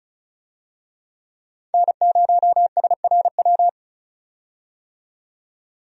Morse code, heard as a keyed tone, about 34 wpm, 700 Hz. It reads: N0SRW
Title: USA Call Signs - 5 Characters - 35wpm